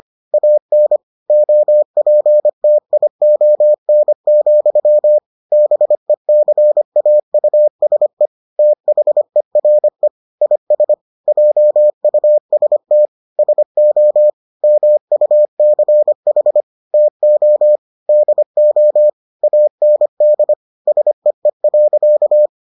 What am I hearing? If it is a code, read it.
Morse code, heard as a keyed tone, about 25 wpm, 600 Hz. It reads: AN OPTION, BECAUSE THERE IS JUST SO MUCH TO DO AND SEE.